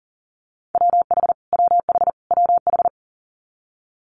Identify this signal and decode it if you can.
Morse code, heard as a keyed tone, about 40 wpm, 700 Hz. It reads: WH WH WH
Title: Multi-letter Phonemes Repeated 3x 40wpm